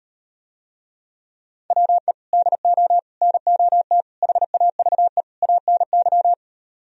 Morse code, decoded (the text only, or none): WE DO NOT HAVE ANY